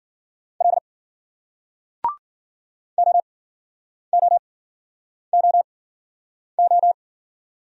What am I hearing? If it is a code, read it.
Morse code, heard as a keyed tone, about 54 wpm, spaced as for 9 wpm, 700 Hz. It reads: L OOOO